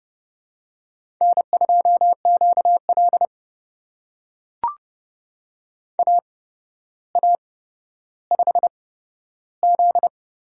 Morse code, heard as a keyed tone, about 30 wpm, spaced as for 13 wpm, 700 Hz. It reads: N2QL AA5Z
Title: USA Call Signs - 4 Characters - 8x Character Spacing - 30wpm